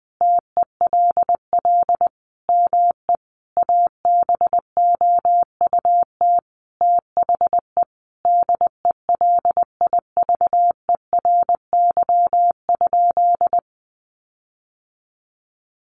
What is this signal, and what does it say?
Morse code, heard as a keyed tone, about 20 wpm, 700 Hz. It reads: TELL ME ABOUT THE DELIVERY?
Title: Sentences from Top 1500 Words 20wpm